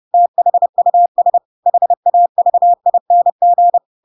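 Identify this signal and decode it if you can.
Morse code, heard as a keyed tone, about 30 wpm, 700 Hz. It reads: THUS HAVING
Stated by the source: Chapter 2 of The Adventures of Pinocchio read as a book in Morse code (CW) at 30 wpm, by ik1hge.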